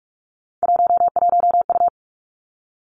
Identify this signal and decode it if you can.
Morse code, heard as a keyed tone, about 46 wpm, 700 Hz. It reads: 11U
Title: Three Challenging Letter-Number Combinations 45wpm